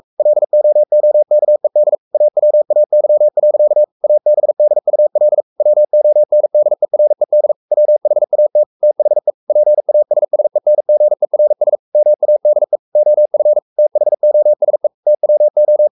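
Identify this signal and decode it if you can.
Morse code, heard as a keyed tone, about 44 wpm, 600 Hz. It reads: LOOKED AWAY. ABDUL WONDERED WHAT THE PASSENGERS MADE OF THOSE TWO
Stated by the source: Morse code at 43 wpm sending Ken Follett's novel Never in English, from CW med Gnister.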